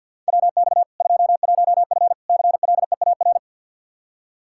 Morse code, heard as a keyed tone, about 50 wpm, 700 Hz. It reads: WX 21F CLEAR
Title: K1USN SST QSO Practice 50wpm